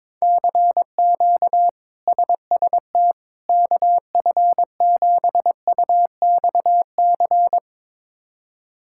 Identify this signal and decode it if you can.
Morse code, heard as a keyed tone, about 22 wpm, 700 Hz. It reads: CQ SST KF7UXC
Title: K1USN SST QSO Practice - 22wpm